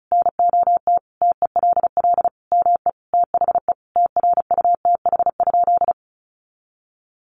Morse code, heard as a keyed tone, about 34 wpm, 700 Hz. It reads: NOT TELL ME THE TRUTH?
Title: Sentences from Top 800 Words 35wpm